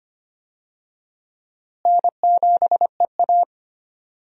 Morse code, heard as a keyed tone, about 25 wpm, 700 Hz. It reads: N7EA
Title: USA Call Signs - 4 Characters - Repeated 2x - 25wpm